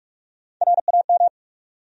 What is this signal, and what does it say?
Morse code, heard as a keyed tone, about 46 wpm, 700 Hz. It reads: RAM
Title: USA Call Sign Postfixes - 3 Letters 45wpm